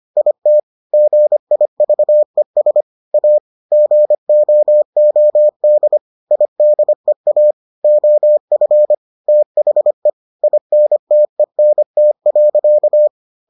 Morse code, heard as a keyed tone, about 25 wpm, 600 Hz. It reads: IT GIVES A GOOD IDEA OF THE INTENT.